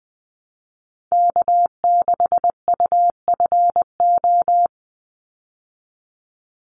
Morse code, heard as a keyed tone, about 20 wpm, 700 Hz. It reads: K6UFO
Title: Call Signs Repeated 3x - 20wpm